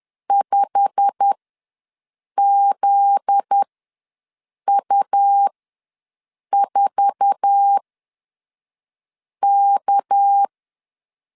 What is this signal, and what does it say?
Morse code, heard as a keyed tone, about 10 wpm, spaced as for 5 wpm, 800 Hz. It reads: 5ZU4K
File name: LEZIONI / 9 2 ws 10wpm 4130